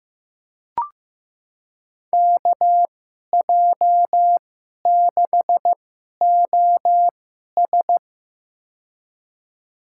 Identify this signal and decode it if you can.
Morse code, heard as a keyed tone, about 15 wpm, spaced as for 11 wpm, 700 Hz. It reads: KJ6OS
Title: USA Call Signs - 2x Character Spacing - 15wpm